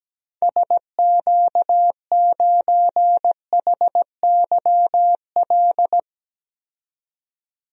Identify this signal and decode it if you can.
Morse code, heard as a keyed tone, about 17 wpm, 700 Hz. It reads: SQ9HYL
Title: DXCC-100 Call Signs - 6 Characters - 17wpm